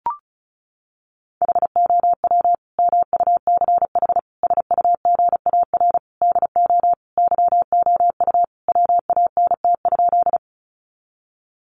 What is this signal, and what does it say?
Morse code, heard as a keyed tone, about 34 wpm, 700 Hz. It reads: HOW MUCH SUGAR DO YOU WANT?
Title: Sentences from Top 1200 Words 35wpm